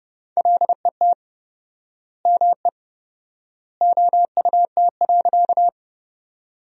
Morse code, heard as a keyed tone, about 30 wpm, 700 Hz. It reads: LET ME OUT.
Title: 4x Word-Spacing - Sentences from Top 300 Words - 30wpm